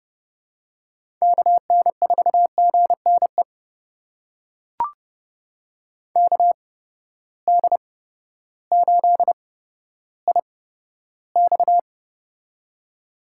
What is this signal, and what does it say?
Morse code, heard as a keyed tone, about 30 wpm, spaced as for 14 wpm, 700 Hz. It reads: KN4GNE KD8IX
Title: USA Call Signs - 8x Character Spacing - 30wpm